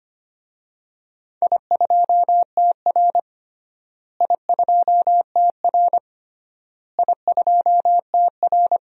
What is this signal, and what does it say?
Morse code, heard as a keyed tone, about 25 wpm, 700 Hz. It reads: I2TR I2TR I2TR
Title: DXCC-100 Call Signs - 4 Characters - Repeated 3x - 25wpm